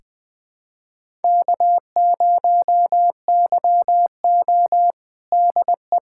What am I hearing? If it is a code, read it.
Morse code, heard as a keyed tone, about 21 wpm, 700 Hz. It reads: K0YO DE
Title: USA Callsign DE Callsign - 4 Characters - 20wpm